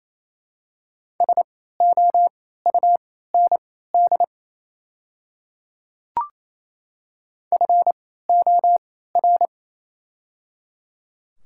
Morse code, heard as a keyed tone, about 27 wpm, spaced as for 16 wpm, 700 Hz. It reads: SOUND FOR
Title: 3x Character Spacing - Top 100 Words - 28wpm